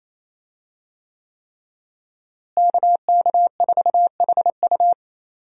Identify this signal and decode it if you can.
Morse code, heard as a keyed tone, about 28 wpm, 700 Hz. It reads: KK4HU
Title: USA Call Signs - 5 Characters - Repeated 3x - 28wpm